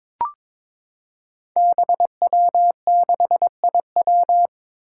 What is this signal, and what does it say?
Morse code, heard as a keyed tone, about 23 wpm, 700 Hz. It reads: BW6IW
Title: Five Challenging Letter-Number Combinations 22wpm